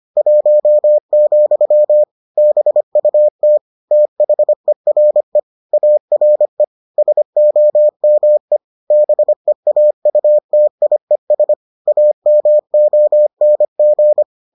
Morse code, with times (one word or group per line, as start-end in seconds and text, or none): 0.2-2.0 1,
2.4-3.6 BUT
3.9-5.4 THERE
5.7-6.6 ARE
7.0-8.6 SOME
8.9-11.5 BEAUTIES
11.9-14.2 AMONG